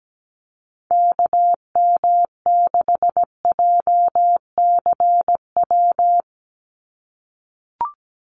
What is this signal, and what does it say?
Morse code, heard as a keyed tone, about 17 wpm, 700 Hz. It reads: KM6JCW
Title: USA Call Signs 17wpm